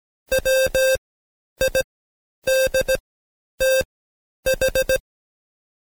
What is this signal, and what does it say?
Morse code, heard as a keyed tone, about 17 wpm, spaced as for 9 wpm, 530 Hz. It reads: WIDTH
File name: Code Quick / AudioDownld2 / Seg131-Stepper4 9WPM